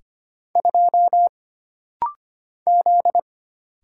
Morse code, heard as a keyed tone, about 25 wpm, 700 Hz. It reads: 2 Z
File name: Single Character (Letter:Number:Punctuation:Prosign) - Rapid-Fire - 25wpm